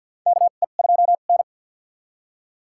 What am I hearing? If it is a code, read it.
Morse code, heard as a keyed tone, about 47 wpm, spaced as for 35 wpm, 700 Hz. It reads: KE2N
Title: USA Call Signs - 4 Characters - 2x Character Spacing - 50wpm